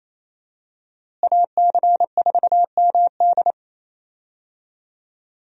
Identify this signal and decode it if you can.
Morse code, heard as a keyed tone, about 28 wpm, 700 Hz. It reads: AC4MD
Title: USA Call Signs - 5 Characters - Repeated 3x - 28wpm